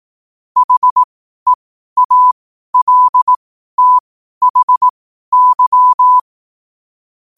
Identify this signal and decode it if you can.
Morse code, heard as a keyed tone, about 18 wpm, spaced as for 12 wpm, 1000 Hz. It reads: HEALTHY
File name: qotdfiles / 12wpm